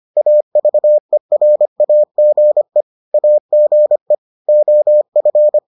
Morse code, heard as a keyed tone, about 25 wpm, 600 Hz. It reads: AVERAGE AGE OF